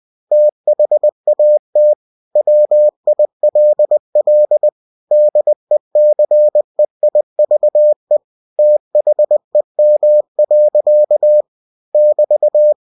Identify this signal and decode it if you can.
Morse code, heard as a keyed tone, about 20 wpm, 600 Hz. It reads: THAT WILL DECEIVE THEM. <BT>
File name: cw-books / winnie-the-pooh / winnie-20wpm-page-0006